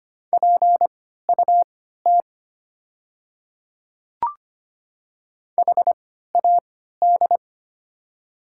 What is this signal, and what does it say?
Morse code, heard as a keyed tone, about 25 wpm, spaced as for 14 wpm, 690 Hz. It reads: PUT HAD
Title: 3x Character Spacing - Top 500 Words - 25wpm